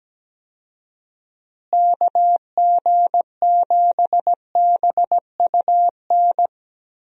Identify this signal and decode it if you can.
Morse code, heard as a keyed tone, about 17 wpm, 700 Hz. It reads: KG7BUN
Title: USA Call Signs - 6 Characters - Repeated 2x - 17wpm